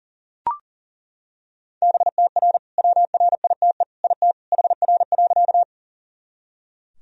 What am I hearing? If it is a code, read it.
Morse code, heard as a keyed tone, about 40 wpm, 700 Hz. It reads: BTR WRITE IT HR.
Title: Sentences from Top 400 Words and Abbreviations 40wpm